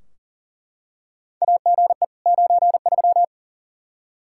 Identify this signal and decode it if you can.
Morse code, heard as a keyed tone, about 40 wpm, 700 Hz. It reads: AGE 93
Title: K1USN SST QSO Practice 40wpm